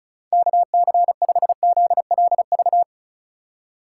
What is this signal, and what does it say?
Morse code, heard as a keyed tone, about 36 wpm, 700 Hz. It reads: KC5ZLV
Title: USA Call Signs - 6 Characters - 2x Character Spacing - 35wpm